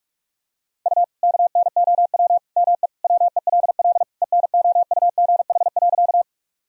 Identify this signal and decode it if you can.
Morse code, heard as a keyed tone, about 46 wpm, 700 Hz. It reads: U KNOW ME WELL ENOUGH.
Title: Sentences from Top 400 Words and Abbreviations 45wpm